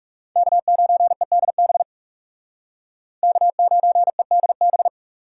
Morse code, heard as a keyed tone, about 38 wpm, 700 Hz. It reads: K9EDB K9EDB